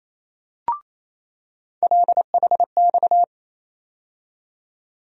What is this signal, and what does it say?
Morse code, heard as a keyed tone, about 28 wpm, 700 Hz. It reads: LHX
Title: USA Call Sign Postfixes - 3 Letters 28wpm